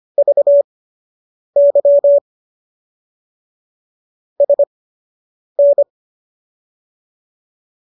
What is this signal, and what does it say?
Morse code, heard as a keyed tone, about 25 wpm, spaced as for 8 wpm, 570 Hz. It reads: VY SN